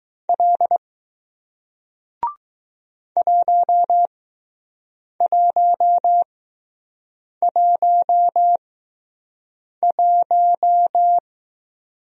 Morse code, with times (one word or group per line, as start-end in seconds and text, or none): 0.3-0.8 L
3.2-11.2 1111